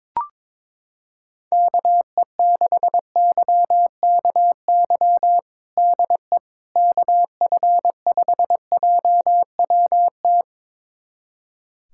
Morse code, heard as a keyed tone, about 22 wpm, 700 Hz. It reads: KE6YKY DE KF5JWT
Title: USA Callsign DE Callsign - 6 Characters - 22wpm